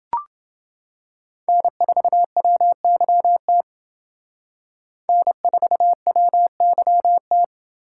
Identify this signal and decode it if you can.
Morse code, heard as a keyed tone, about 30 wpm, 700 Hz. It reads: N4WYT N4WYT